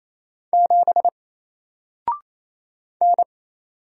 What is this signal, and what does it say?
Morse code, heard as a keyed tone, about 28 wpm, 700 Hz. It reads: 7 N
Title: Single Letter-Number Alt 28wpm